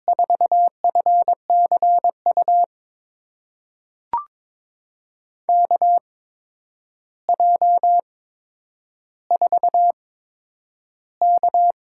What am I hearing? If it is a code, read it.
Morse code, heard as a keyed tone, about 22 wpm, spaced as for 9 wpm, 700 Hz. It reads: 4FCU KJ4K
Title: USA Call Signs - 6 Characters - 8x Character Spacing - 22wpm